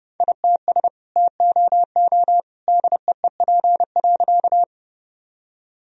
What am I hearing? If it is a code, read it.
Morse code, heard as a keyed tone, about 30 wpm, 700 Hz. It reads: ITS TOO DEEP.
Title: Sentences from Top 1000 Words - Review Part 3 - 30wpm